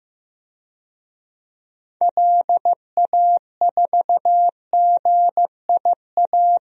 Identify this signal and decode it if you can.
Morse code, heard as a keyed tone, about 16 wpm, 700 Hz. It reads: LA4GIA
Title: DXCC-100 Call Signs - 6 Characters - Repeated 2x - 15wpm